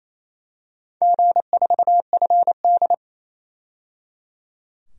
Morse code, heard as a keyed tone, about 28 wpm, 700 Hz. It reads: G4FD
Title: DXCC-100 Call Signs - 4 Characters - 28wpm